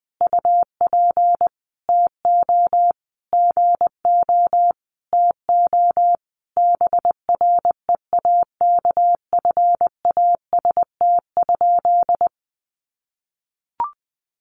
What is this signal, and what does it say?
Morse code, heard as a keyed tone, about 19 wpm, 700 Hz. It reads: UP TO GO TO BREAKFAST?
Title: Sentences from Top 1500 Words 20wpm